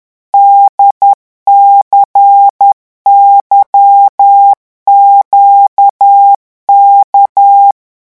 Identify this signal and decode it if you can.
Morse code, heard as a keyed tone, about 10 wpm, 800 Hz. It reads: DCYQK
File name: morse mp3 / revision3-vitesse 10 mots